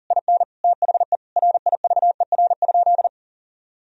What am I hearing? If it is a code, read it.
Morse code, heard as a keyed tone, about 40 wpm, 700 Hz. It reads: IN THE RIVER?
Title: Sentences from Top 1000 Words - Review Part 4 - 40wpm